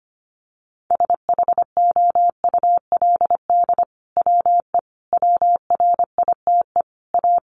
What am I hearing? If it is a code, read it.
Morse code, heard as a keyed tone, about 25 wpm, 700 Hz. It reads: SHOULD WE WRITE A
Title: Sentences from Top 1300 Words 25wpm